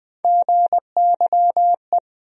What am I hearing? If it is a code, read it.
Morse code, heard as a keyed tone, about 20 wpm, 700 Hz. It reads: GYE
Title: USA Call Sign Postfixes - 3 Letters 20wpm